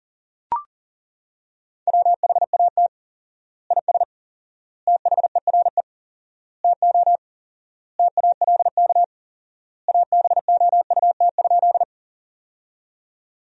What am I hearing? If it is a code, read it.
Morse code, heard as a keyed tone, about 40 wpm, 700 Hz. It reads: WHAT IS THERE TO TALK ABOUT?
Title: 4x Word-Spacing - Sentences from Top 400 Words - 40wpm